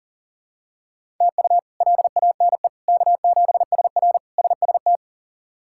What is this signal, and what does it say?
Morse code, heard as a keyed tone, about 40 wpm, 700 Hz. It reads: TU LANE K7SR SST
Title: K1USN SST QSO Practice - 40wpm